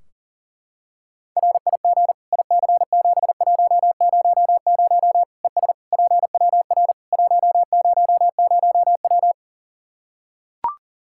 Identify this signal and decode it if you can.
Morse code, heard as a keyed tone, about 40 wpm, 700 Hz. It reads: RIG IC7100 ES PWR 100W
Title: Practice QSO - Exchange 2 - Info - 40wpm